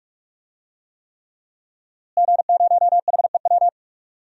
Morse code, heard as a keyed tone, about 43 wpm, 700 Hz. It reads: G0HEW
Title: DXCC-100 Call Signs - 5 Characters - Repeated 3x - 45wpm